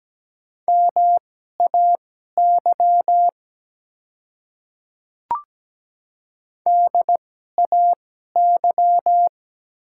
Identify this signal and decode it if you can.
Morse code, heard as a keyed tone, about 17 wpm, spaced as for 12 wpm, 700 Hz. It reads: MAY DAY
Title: 2x Character Spacing - Top 100 Words - 17wpm